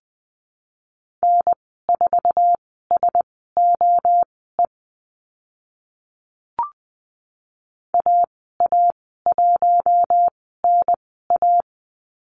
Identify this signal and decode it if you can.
Morse code, heard as a keyed tone, about 20 wpm, spaced as for 15 wpm, 700 Hz. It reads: N4SOE AA1NA
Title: 062 Lesson 11g - Letter h - Call Signs 20wpm